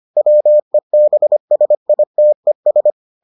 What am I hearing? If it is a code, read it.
Morse code, heard as a keyed tone, about 25 wpm, 600 Hz. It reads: WEBSITES